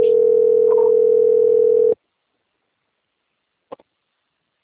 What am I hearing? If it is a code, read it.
Ringback tone (North America), 440 + 480 Hz.
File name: out- - - - - .13837